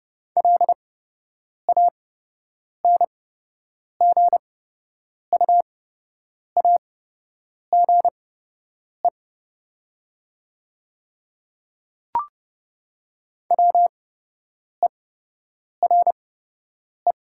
Morse code, heard as a keyed tone, about 30 wpm, spaced as for 8 wpm, 700 Hz. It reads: LANGUAGE WERE
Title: 8x Character Spacing - Top 500 Words - 30wpm